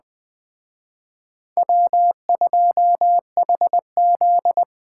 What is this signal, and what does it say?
Morse code, heard as a keyed tone, about 20 wpm, 700 Hz. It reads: W2HZ
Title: Speed Racing - USA 1x2 Call Signs 17wpm